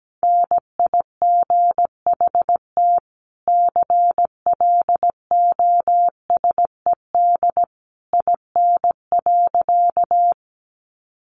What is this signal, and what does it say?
Morse code, heard as a keyed tone, about 17 wpm, 700 Hz. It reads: NIGHT CLOSED IN.
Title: Sentences from Top 1400 Words 17wpm